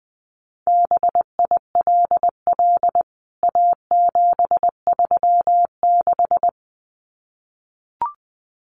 Morse code, heard as a keyed tone, about 20 wpm, 700 Hz. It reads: BILL A736
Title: CWT Exchange 20wpm